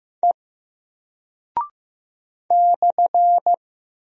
Morse code, heard as a keyed tone, about 15 wpm, 700 Hz. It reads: E /
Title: Single Character (Letter:Number:Punctuation:Prosign) 15wpm